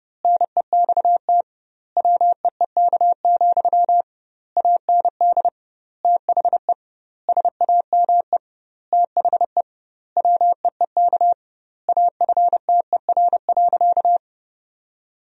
Morse code, heard as a keyed tone, about 30 wpm, 700 Hz. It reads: NEXT WEEK, AND THE SAME THE WEEK AFTER.
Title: 2x Word-Spacing - Sentences from Top 500 Words - 30wpm